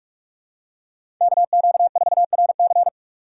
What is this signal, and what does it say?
Morse code, heard as a keyed tone, about 47 wpm, 690 Hz. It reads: KQ4RC
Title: USA Call Signs - 5 Characters - 4x Character Spacing - 45wpm